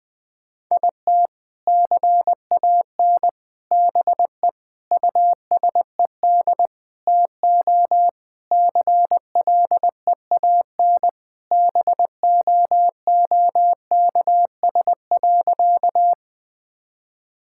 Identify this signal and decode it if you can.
Morse code, heard as a keyed tone, about 20 wpm, 700 Hz. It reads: IT CAN BE USED TO CLEAN BOOKS.
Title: Sentences with 4-Word Phrases - Spoken Beginning - 20wpm